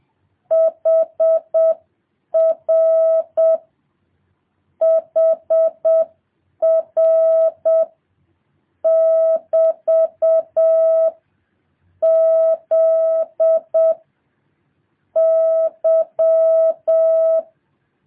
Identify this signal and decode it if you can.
Morse code, heard as a keyed tone, about 7 wpm, 650 Hz. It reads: HR HR <BT> Z Y
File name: jh8cbh / 2023cw